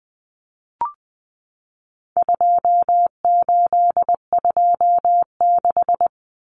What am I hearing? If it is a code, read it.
Morse code, heard as a keyed tone, about 20 wpm, 700 Hz. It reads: 2826
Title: Numbers - 4-Digit 20wpm